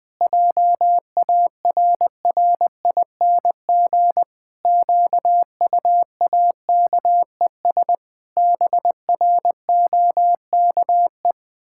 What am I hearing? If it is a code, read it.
Morse code, heard as a keyed tone, about 21 wpm, 700 Hz. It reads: JARRING QUAKES BROKE
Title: Pangram Challenge v2 - 20wpm